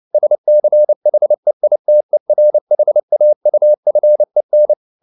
Morse code, heard as a keyed tone, about 29 wpm, 600 Hz. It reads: SCHEITERHAUFEN